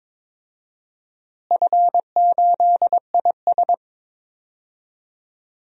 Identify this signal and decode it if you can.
Morse code, heard as a keyed tone, about 22 wpm, 700 Hz. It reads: F8IS
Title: DXCC-100 Call Signs - 4 Characters - Repeated 3x - 22wpm